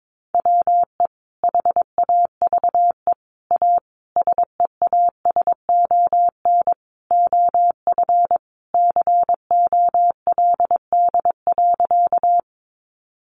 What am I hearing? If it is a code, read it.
Morse code, heard as a keyed tone, about 22 wpm, 700 Hz. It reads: WE HAVE A SEASON OF COLD.
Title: Sentences from Top 1400 Words 22wpm